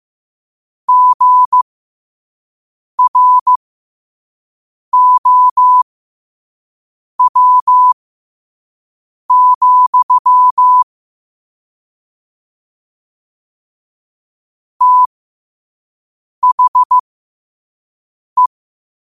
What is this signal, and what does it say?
Morse code, heard as a keyed tone, about 15 wpm, spaced as for 5 wpm, 1000 Hz. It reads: GROW, THE
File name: qotdfiles / 5wpm